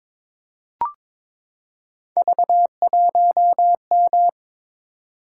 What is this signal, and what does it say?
Morse code, heard as a keyed tone, about 22 wpm, 700 Hz. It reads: V1M
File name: practice / 22wpm / Three Letter-Number Combinations - 22wpm